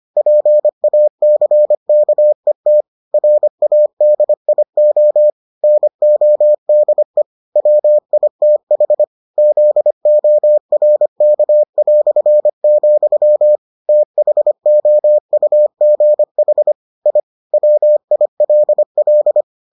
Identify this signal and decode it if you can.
Morse code, heard as a keyed tone, about 25 wpm, 600 Hz. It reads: PACKET RADIO NODE WITH ZORK", THOUGH I WILL